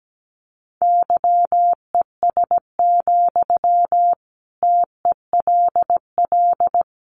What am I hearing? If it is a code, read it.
Morse code, heard as a keyed tone, about 17 wpm, 700 Hz. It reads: YES, TELL
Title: Sentences from Top 500 Words 17wpm